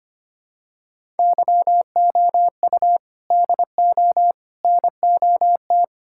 Morse code, heard as a keyed tone, about 25 wpm, 700 Hz. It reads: YOU DO NOT
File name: practice / 25wpm / Sentences with 3-Word Phrases - 25wpm